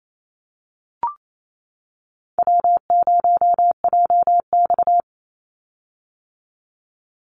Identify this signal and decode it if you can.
Morse code, heard as a keyed tone, about 27 wpm, 700 Hz. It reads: W0JX
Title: Easy Call Signs - 28wpm